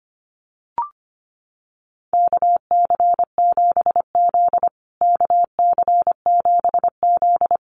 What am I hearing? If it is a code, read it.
Morse code, heard as a keyed tone, about 25 wpm, 700 Hz. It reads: KC7Z KC7Z
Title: USA Call Signs Repeated 3x 25wpm